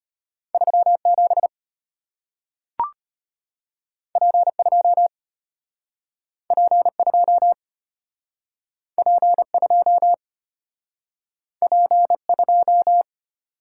Morse code, heard as a keyed tone, about 34 wpm, 700 Hz. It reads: 37 P2 P2 P2 P2